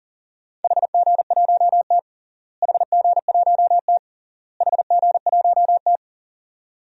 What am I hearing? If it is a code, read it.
Morse code, heard as a keyed tone, about 40 wpm, 700 Hz. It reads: HG1T HG1T HG1T
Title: DXCC-100 Call Signs - 4 Characters - Repeated 3x - 40wpm